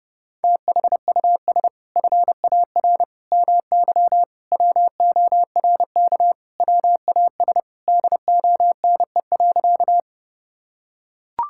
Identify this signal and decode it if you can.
Morse code, heard as a keyed tone, about 30 wpm, 700 Hz. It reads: THUS FAR MY WORK WAS DONE.
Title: Sentences from Top 1000 Words - Review Part 5 - 30wpm